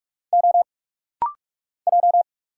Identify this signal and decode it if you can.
Morse code, heard as a keyed tone, about 46 wpm, 700 Hz. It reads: O J
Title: Single Letters - Rapid-Fire 45wpm